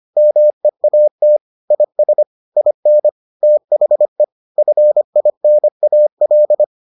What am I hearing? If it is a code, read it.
Morse code, heard as a keyed tone, about 25 wpm, 600 Hz. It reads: MEAT IS IN THE FINAL